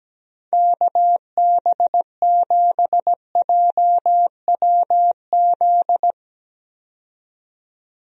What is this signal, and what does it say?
Morse code, heard as a keyed tone, about 17 wpm, 700 Hz. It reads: KB7JWZ
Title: USA Call Signs - 6 Characters - 17wpm